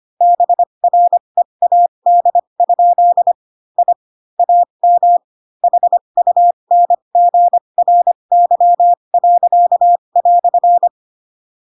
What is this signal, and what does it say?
Morse code, heard as a keyed tone, about 25 wpm, spaced as for 22 wpm, 700 Hz. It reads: BREAD? I AM HUNGRY."
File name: Pinocchio / Pinocchio ch6 18 25 ik1hge